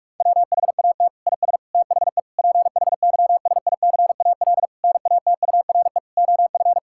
Morse code, heard as a keyed tone, about 44 wpm, 700 Hz. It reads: WHAT IS THE PHYSICAL NATURE OF
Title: Sentences from Top 1000 Words - Review Part 6 - 45wpm